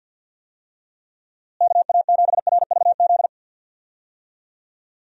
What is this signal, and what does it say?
Morse code, heard as a keyed tone, about 53 wpm, 700 Hz. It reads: KA7RVZ
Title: USA Call Signs - 6 Characters - Repeated 2x - 50wpm